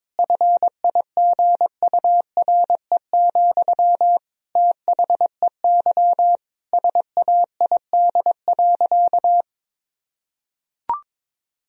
Morse code, heard as a keyed tone, about 22 wpm, 700 Hz. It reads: FIGURE, THEY SAID.
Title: Sentences from Top 1000 Words - Review Part 2 - 22wpm